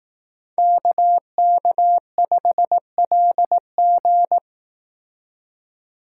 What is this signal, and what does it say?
Morse code, heard as a keyed tone, about 18 wpm, 700 Hz. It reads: KK5LG
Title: Speed Racing - USA 2x2 Call Signs 15wpm